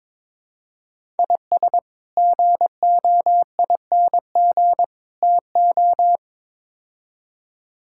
Morse code, heard as a keyed tone, about 23 wpm, 700 Hz. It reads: IS GOING TO
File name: practice / 22wpm / Build A Phrase - 5-Word Phrases - 22wpm